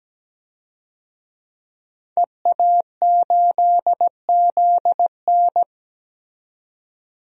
Morse code, heard as a keyed tone, about 17 wpm, 700 Hz. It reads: EA8ZN
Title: DXCC-100 Call Signs - 5 Characters - 17wpm